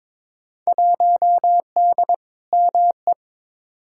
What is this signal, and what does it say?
Morse code, heard as a keyed tone, about 22 wpm, 700 Hz. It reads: 1D ME
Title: Field Day Exchanges - 22wpm